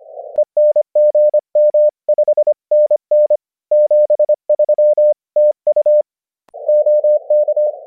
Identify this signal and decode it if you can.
Morse code, heard as a keyed tone, about 25 wpm, 600 Hz. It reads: ENG M 5NN 73 TU OK